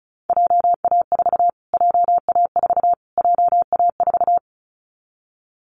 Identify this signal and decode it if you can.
Morse code, heard as a keyed tone, about 34 wpm, 700 Hz. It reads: JA4 JA4 JA4
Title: Callsign Prefixes for Top 25 DXCC - Repeated 3x - 35wpm